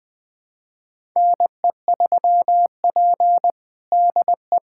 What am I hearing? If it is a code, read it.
Morse code, heard as a keyed tone, about 20 wpm, 700 Hz. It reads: NE3P DE
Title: USA Callsign DE Callsign - 4 Characters - 20wpm